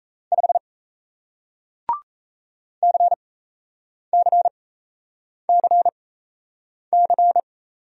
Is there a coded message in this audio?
Yes: Morse code from 0.3 to 7.4 s, heard as a keyed tone, about 37 wpm, spaced as for 8 wpm, 700 Hz. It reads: 5 CCCC